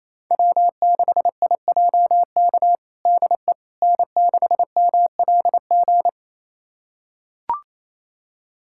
Morse code, heard as a keyed tone, about 27 wpm, 700 Hz. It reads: W6IJK DE N6MLG
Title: USA Callsign DE Callsign - 5 Characters - 28wpm